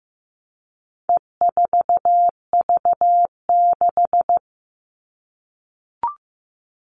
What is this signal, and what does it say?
Morse code, heard as a keyed tone, about 15 wpm, 700 Hz. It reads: E4V6
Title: All Cut Numbers - 4-Digit 15wpm